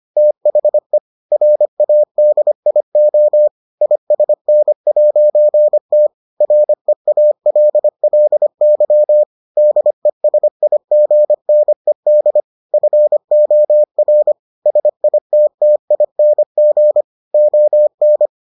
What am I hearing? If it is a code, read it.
Morse code, heard as a keyed tone, about 25 wpm, 600 Hz. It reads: THE RADIO ISN'T REALLY DESIGNED FOR SITTING ON